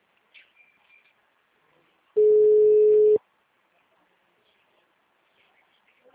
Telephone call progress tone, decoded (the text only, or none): ringback tone